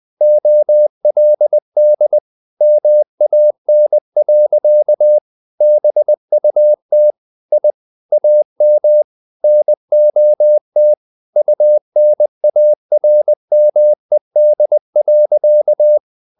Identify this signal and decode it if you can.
Morse code, heard as a keyed tone, about 20 wpm, 600 Hz. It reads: OLD MAN. BUT I AM NOT UNARMED.
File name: the old man and the sea part 2 20wpm